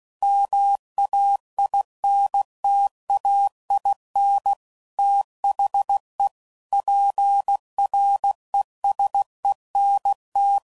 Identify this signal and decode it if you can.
Morse code, heard as a keyed tone, about 16 wpm, 800 Hz. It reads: MAINTAIN THE PRESENT
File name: CW Code Groups @15WPM / Technical Subjects / XL